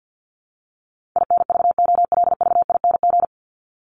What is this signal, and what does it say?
Morse code, heard as a keyed tone, about 50 wpm, 700 Hz. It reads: INVOLVING
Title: ING Suffix - Top 100 Words with Base Words 50wpm